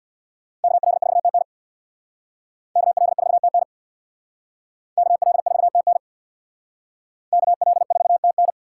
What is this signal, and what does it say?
Morse code, heard as a keyed tone, about 60 wpm, 700 Hz. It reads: KL4TN KL4TN KL4TN KL4TN
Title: Speed Racing - USA 2x2 Call Signs 50wpm